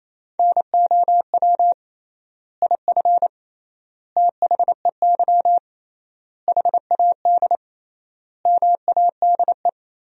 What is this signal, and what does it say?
Morse code, heard as a keyed tone, about 28 wpm, 700 Hz. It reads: NOW IF THEY HAD MADE
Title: 3x Word-Spacing - Sentences from Top 500 Words - 28wpm